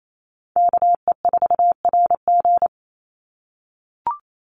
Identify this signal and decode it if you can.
Morse code, heard as a keyed tone, about 28 wpm, 700 Hz. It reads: KE4RG
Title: Call Signs - 28wpm